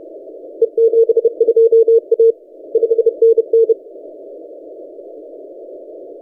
Morse code, heard as a keyed tone, about 30 wpm, 460 Hz. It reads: E72A 5NN
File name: posts / 4w-g3zem-30cw- -1525z